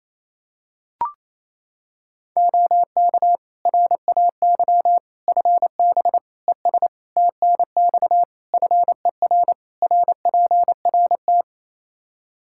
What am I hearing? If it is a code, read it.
Morse code, heard as a keyed tone, about 28 wpm, 700 Hz. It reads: OK RAY FB ES TNX FER RPRT
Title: Practice QSO - Exchange 2 - Info - 28wpm